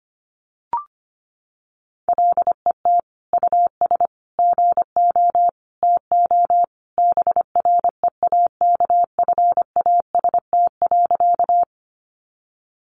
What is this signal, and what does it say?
Morse code, heard as a keyed tone, about 25 wpm, 700 Hz. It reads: LET US GO TO BREAKFAST.
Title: Sentences from Top 1500 Words 25wpm